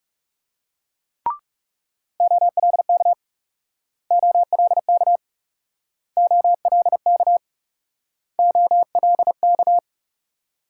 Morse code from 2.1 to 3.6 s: OLK

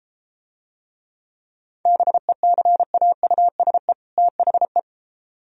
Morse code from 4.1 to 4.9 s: THE

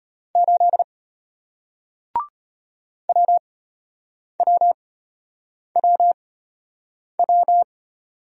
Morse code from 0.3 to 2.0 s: 8

